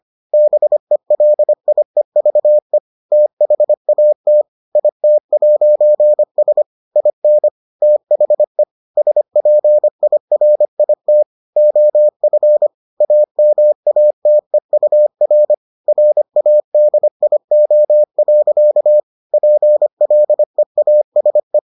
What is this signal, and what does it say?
Morse code, heard as a keyed tone, about 25 wpm, 600 Hz. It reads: BELIEVE THAT IT'S IN THE SPIRIT OF AMATEUR RADIO. PLEASE